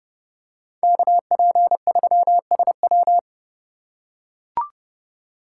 Morse code, heard as a keyed tone, about 30 wpm, 700 Hz. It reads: KP3SW